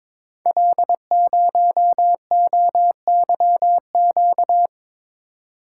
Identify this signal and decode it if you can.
Morse code, heard as a keyed tone, about 22 wpm, 700 Hz. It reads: L0OYQ
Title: Code Groups - 5 Letter:Numbers - 22wpm